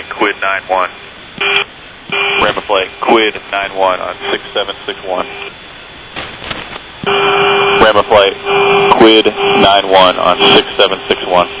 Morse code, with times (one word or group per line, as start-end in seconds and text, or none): none